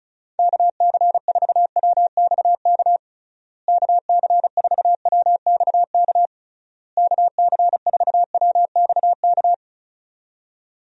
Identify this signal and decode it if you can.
Morse code, heard as a keyed tone, about 34 wpm, 700 Hz. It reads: KC4WXK KC4WXK KC4WXK
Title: USA Call Signs - 6 Characters - Repeated 3x - 35wpm